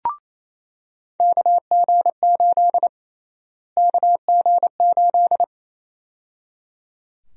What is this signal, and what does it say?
Morse code, heard as a keyed tone, about 28 wpm, 700 Hz. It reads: KG8 KG8
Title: USA Call Sign Prefixes - Repeated 2x - 28wpm